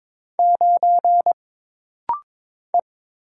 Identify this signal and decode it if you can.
Morse code, heard as a keyed tone, about 22 wpm, 700 Hz. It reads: 9 E
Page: This rapid-fire timing is geared to push your ICR (Instant Character Recognition) skills.